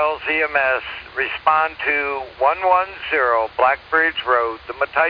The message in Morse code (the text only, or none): none